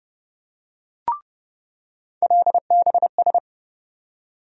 Morse code, heard as a keyed tone, about 30 wpm, 700 Hz. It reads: LBS
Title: USA Call Sign Postfixes - 3 Letters 30wpm